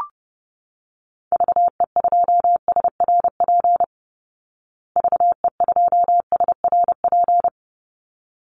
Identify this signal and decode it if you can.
Morse code, heard as a keyed tone, about 30 wpm, 700 Hz. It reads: VE2SRP VE2SRP
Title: Call Signs Repeated 3x - 30wpm